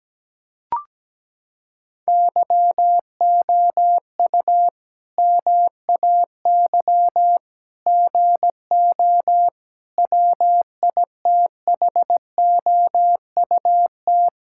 Morse code, heard as a keyed tone, about 17 wpm, 700 Hz. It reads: YOU MAY GO WITHOUT
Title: Sentences from Top 1600 Words 17wpm